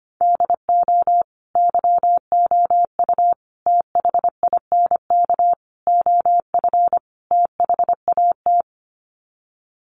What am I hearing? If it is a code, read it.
Morse code, heard as a keyed tone, about 25 wpm, 700 Hz. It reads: DO YOU THINK OF THAT
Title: Sets of 5 Words Encore2 25wpm